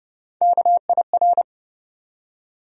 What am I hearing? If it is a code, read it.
Morse code, heard as a keyed tone, about 30 wpm, 700 Hz. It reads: KIR
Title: USA Call Sign Postfixes - 3 Letters 30wpm